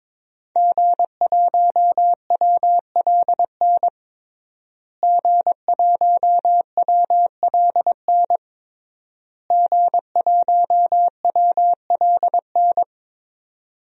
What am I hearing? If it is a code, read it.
Morse code, heard as a keyed tone, about 22 wpm, 700 Hz. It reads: G1WLN G1WLN G1WLN
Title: DXCC-100 Call Signs - 5 Characters - Repeated 3x - 22wpm